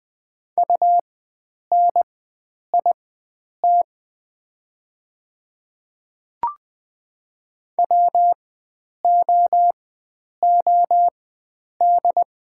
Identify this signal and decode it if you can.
Morse code, heard as a keyed tone, about 20 wpm, spaced as for 9 wpm, 690 Hz. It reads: UNIT WOOD
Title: 4x Character Spacing - Top 500 Words - 20wpm